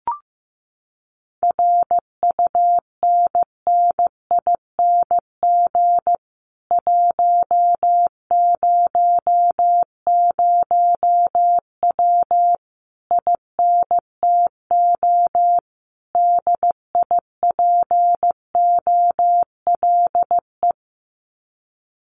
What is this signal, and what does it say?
Morse code, heard as a keyed tone, about 15 wpm, 700 Hz. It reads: RUNNING 100W INTO DIPOLE
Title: Common QSO Phrases Encore 15wpm